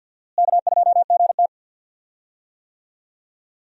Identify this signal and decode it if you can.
Morse code, heard as a keyed tone, about 53 wpm, 700 Hz. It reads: K2GT
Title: USA Call Signs - 4 Characters - Repeated 3x - 50wpm